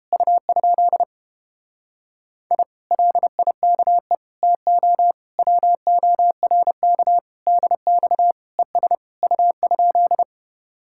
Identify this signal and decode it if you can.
Morse code, heard as a keyed tone, about 31 wpm, 700 Hz. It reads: U? I LIKE TO WORK DX ES U?